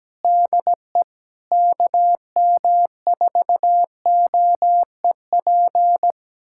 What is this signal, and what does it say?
Morse code, heard as a keyed tone, about 17 wpm, 700 Hz. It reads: DE KM4OEP
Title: USA Callsign DE Callsign - 6 Characters - 17wpm